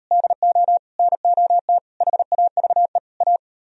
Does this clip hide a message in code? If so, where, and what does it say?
Yes: Morse code from 0.1 to 3.4 s, heard as a keyed tone, about 36 wpm, 700 Hz. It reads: DO NOT HAVE A